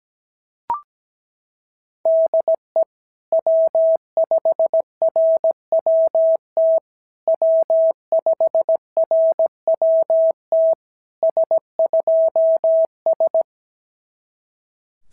Morse code, heard as a keyed tone, about 16 wpm, 650 Hz. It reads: DE W5RWT W5RWT S2S
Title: SOTA QSO Practice 17wpm